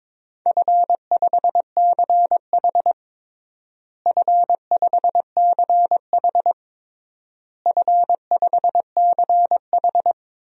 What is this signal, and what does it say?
Morse code, heard as a keyed tone, about 22 wpm, 700 Hz. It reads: F5CH F5CH F5CH
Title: DXCC-100 Call Signs - 4 Characters - Repeated 3x - 22wpm